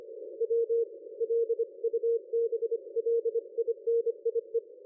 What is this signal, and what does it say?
Morse code, heard as a keyed tone, about 25 wpm, 460 Hz. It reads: W LUBLINIE
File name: dlazaawansowanych / nagraniazpasma / ExpertSDR2 line out 23.01.2020 14.12.00